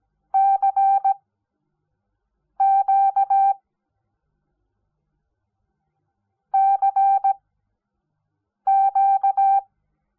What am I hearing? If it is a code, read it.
Morse code, heard as a keyed tone, about 17 wpm, spaced as for 5 wpm, 780 Hz. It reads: CQ CQ
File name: Onlinelehrgang / morsen / DJ4UF Morsen05